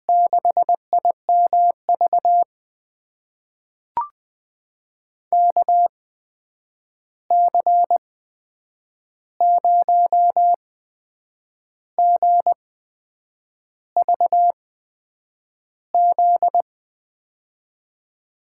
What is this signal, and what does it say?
Morse code, heard as a keyed tone, about 20 wpm, spaced as for 7 wpm, 700 Hz. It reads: 6IMV KC0GVZ
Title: USA Call Signs - 6 Characters - 8x Character Spacing - 20wpm